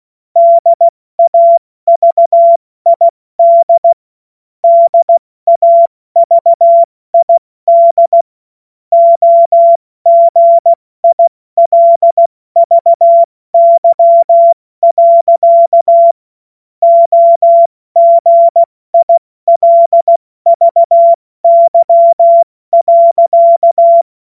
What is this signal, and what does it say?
Morse code, heard as a keyed tone, about 16 wpm, 680 Hz. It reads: DAVID DAVID OGILVY. OGILVY.
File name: Morsekurs / Witze / Witz 27. September 2022 70 F80